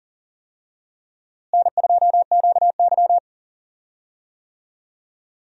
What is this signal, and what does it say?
Morse code, heard as a keyed tone, about 40 wpm, 700 Hz. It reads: N2QY
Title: USA Call Signs - 4 Characters - Repeated 2x - 40wpm